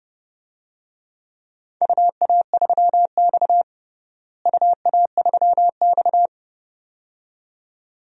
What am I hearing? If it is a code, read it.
Morse code, heard as a keyed tone, about 32 wpm, 700 Hz. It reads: UA3X UA3X
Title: DXCC-100 Call Signs - 4 Characters - Repeated 2x - 30wpm